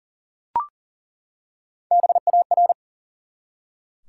Morse code, heard as a keyed tone, about 38 wpm, 700 Hz. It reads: BAR
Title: Three Letter Words - 40wpm